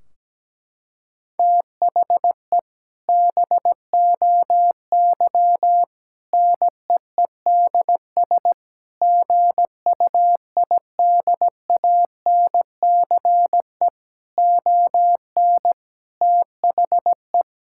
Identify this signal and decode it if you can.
Morse code, heard as a keyed tone, about 17 wpm, 700 Hz. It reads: THE BOY NEEDS GUIDANCE ON THE
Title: Sentences from Top 1600 Words 17wpm